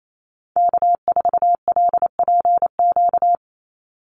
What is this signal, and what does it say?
Morse code, heard as a keyed tone, about 28 wpm, 700 Hz. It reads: K4LPQ
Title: NA SOTA Activator Callsigns 28wpm